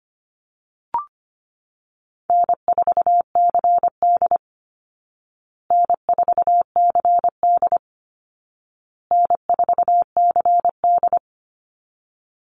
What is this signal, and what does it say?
Morse code, heard as a keyed tone, about 25 wpm, 700 Hz. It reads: N4CD N4CD N4CD
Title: Easy Call Signs Repeated 3x - 25wpm